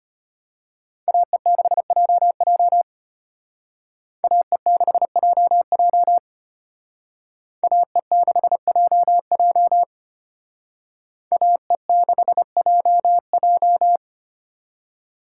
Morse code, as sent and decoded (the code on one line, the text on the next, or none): .- . -.... .--- .--- / .- . -.... .--- .--- / .- . -.... .--- .--- / .- . -.... .--- .---
AE6JJ AE6JJ AE6JJ AE6JJ